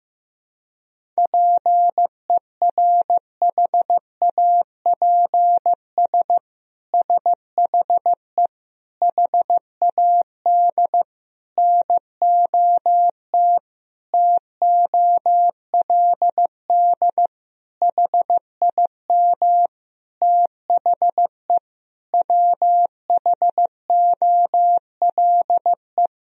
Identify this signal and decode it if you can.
Morse code, heard as a keyed tone, about 15 wpm, 700 Hz. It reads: PERHAPS SHE HAD NOT TOLD HIM THE WHOLE
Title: Sentences from Top 1000 Words - Review Part 1 - 15wpm